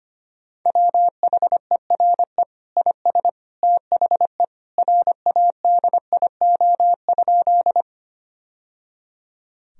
Morse code, heard as a keyed tone, about 25 wpm, 700 Hz. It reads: WHERE IS THE RADIO?
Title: Sentences from Top 1000 Words - Review Part 3 - 25wpm